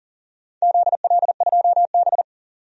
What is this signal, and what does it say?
Morse code, heard as a keyed tone, about 40 wpm, 700 Hz. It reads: ZL2B
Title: DXCC-100 Call Signs - 4 Characters - 40wpm